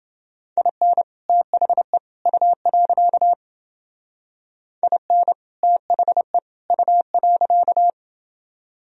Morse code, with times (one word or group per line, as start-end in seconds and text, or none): 0.6-1.0 IN
1.3-2.0 THE
2.3-3.3 U.
4.8-5.3 IN
5.6-6.4 THE
6.7-7.9 U.